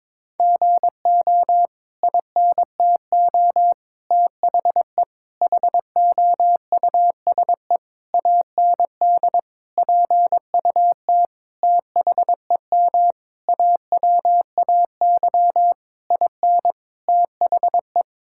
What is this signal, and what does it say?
Morse code, heard as a keyed tone, about 22 wpm, 700 Hz. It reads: GO INTO THE HOUSE AND PUT THEM AWAY IN THE
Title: Sentences from Top 1000 Words - Review Part 5 - 22wpm